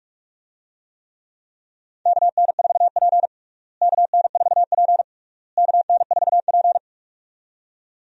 Morse code, heard as a keyed tone, about 43 wpm, 700 Hz. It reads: KN4P KN4P KN4P
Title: USA Call Signs - 4 Characters - Repeated 3x - 45wpm